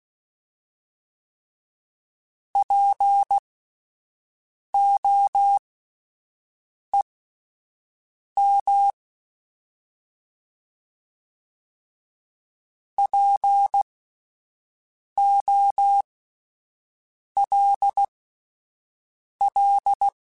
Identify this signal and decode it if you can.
Morse code, heard as a keyed tone, about 16 wpm, spaced as for 5 wpm, 800 Hz. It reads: POEM POLL